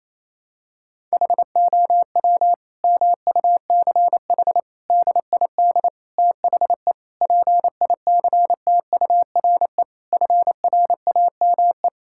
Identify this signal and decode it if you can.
Morse code, heard as a keyed tone, about 28 wpm, 700 Hz. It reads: HOW MUCH DID THE PICTURE FRAME
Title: Sentences from Top 1000 Words - Review Part 6 - 28wpm